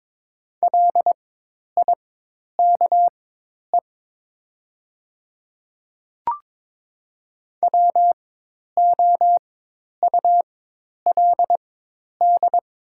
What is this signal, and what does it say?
Morse code, heard as a keyed tone, about 22 wpm, spaced as for 10 wpm, 700 Hz. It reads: LIKE WOULD
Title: 4x Character Spacing - Top 100 Words - 22wpm